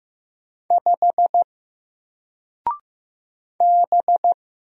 Morse code, heard as a keyed tone, about 15 wpm, 700 Hz. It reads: 5 B
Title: Single Character (Letter:Number:Punctuation:Prosign) 15wpm